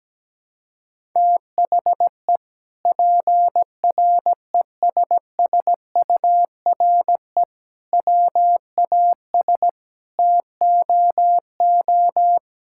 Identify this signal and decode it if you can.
Morse code, heard as a keyed tone, about 17 wpm, 700 Hz. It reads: THE PRESSURE WAS TOO
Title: Sentences from Top 1000 Words - Review Part 6 - 17wpm